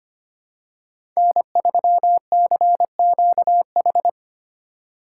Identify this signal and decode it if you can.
Morse code, heard as a keyed tone, about 25 wpm, 700 Hz. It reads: N3CQH
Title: USA Call Signs - 5 Characters - 8x Character Spacing - 25wpm